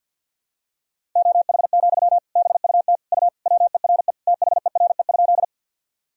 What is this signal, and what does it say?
Morse code, heard as a keyed tone, about 53 wpm, 700 Hz. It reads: OH, BUT U WERE THERE?
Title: Sentences from Top 500 Words and Abbreviations 50wpm